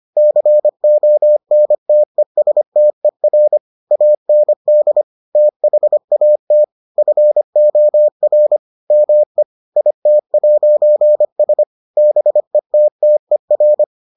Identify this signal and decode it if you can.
Morse code, heard as a keyed tone, about 25 wpm, 600 Hz. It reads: CONTESTER AND THAT FOR ME IT'S BETTER